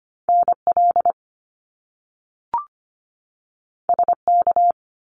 Morse code, heard as a keyed tone, about 25 wpm, 700 Hz. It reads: NL SK
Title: Canadian Provinces and Territories 25wpm